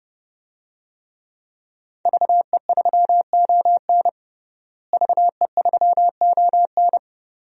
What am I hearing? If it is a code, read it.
Morse code, heard as a keyed tone, about 31 wpm, 700 Hz. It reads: VE3ON VE3ON
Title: DXCC-100 Call Signs - 5 Characters - Repeated 2x - 30wpm